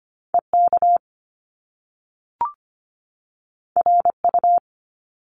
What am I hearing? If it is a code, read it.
Morse code, heard as a keyed tone, about 25 wpm, 700 Hz. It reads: EK RU
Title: Top Two Letter Combinations in Words 25wpm